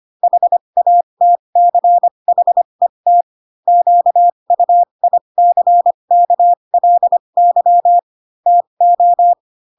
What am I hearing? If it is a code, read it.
Morse code, heard as a keyed tone, about 25 wpm, spaced as for 22 wpm, 700 Hz. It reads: HATCHET QUICKLY TO